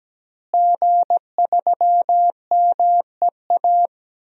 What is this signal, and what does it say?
Morse code, heard as a keyed tone, about 17 wpm, 700 Hz. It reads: G3MEA
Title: DXCC-100 Call Signs - 5 Characters - Repeated 2x - 17wpm